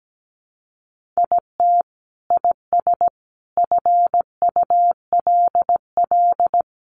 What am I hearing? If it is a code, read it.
Morse code, heard as a keyed tone, about 17 wpm, 700 Hz. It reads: IT IS FULL
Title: Sentences from Top 1500 Words 17wpm